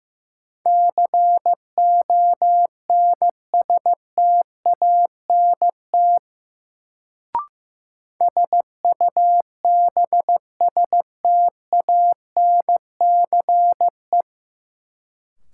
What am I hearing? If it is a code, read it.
Morse code, heard as a keyed tone, about 15 wpm, 700 Hz. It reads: CONSTANT SUBSTANCE
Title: Sentences from Top 1500 Words - New Word Drill Alt - 15wpm